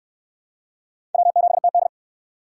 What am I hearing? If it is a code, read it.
Morse code, heard as a keyed tone, about 69 wpm, 700 Hz. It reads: W7TD